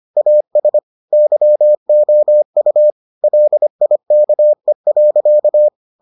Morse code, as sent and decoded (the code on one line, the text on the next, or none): .- ... / -.-- --- ..- / .-.. .. -.- . .-.-.-
AS YOU LIKE.